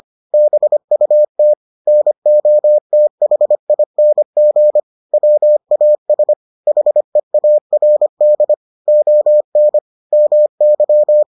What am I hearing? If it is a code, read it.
Morse code, heard as a keyed tone, about 25 wpm, 600 Hz. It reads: BUT NOTHING WAS HEARD ON MY